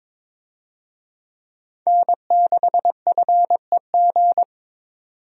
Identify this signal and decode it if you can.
Morse code, heard as a keyed tone, about 22 wpm, 700 Hz. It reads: N6FEG
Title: USA Call Signs - 5 Characters - 22wpm